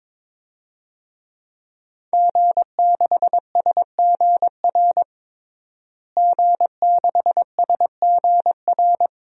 Morse code, heard as a keyed tone, about 22 wpm, 700 Hz. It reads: G6SGR G6SGR
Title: DXCC-100 Call Signs - 5 Characters - Repeated 3x - 22wpm